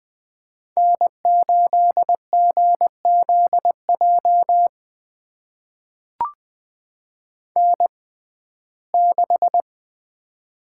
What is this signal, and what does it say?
Morse code, heard as a keyed tone, about 20 wpm, spaced as for 14 wpm, 700 Hz. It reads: N8GZJ N6
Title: USA Call Signs - 5 Characters - 6x Character Spacing - 20wpm